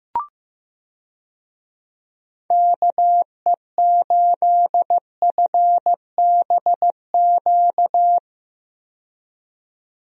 Morse code, heard as a keyed tone, about 15 wpm, 700 Hz. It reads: KE8FBQ
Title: USA Call Signs - 6 Characters - 15wpm